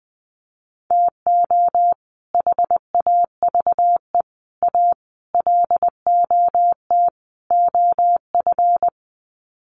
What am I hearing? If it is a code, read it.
Morse code, heard as a keyed tone, about 20 wpm, 700 Hz. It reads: TO HAVE A LOT OF
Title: Sets of 5 Words Encore 20wpm